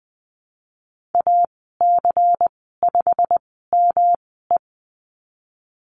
Morse code, heard as a keyed tone, about 20 wpm, spaced as for 15 wpm, 700 Hz. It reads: AC5ME
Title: 122 Lesson 20g - Number 3 - Call Signs 20wpm